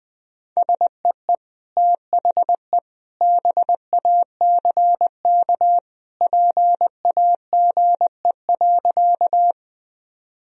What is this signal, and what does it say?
Morse code, heard as a keyed tone, about 20 wpm, 700 Hz. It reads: SEE THE BACK PAGE.
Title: Sentences from Top 1000 Words - Review Part 1 - 20wpm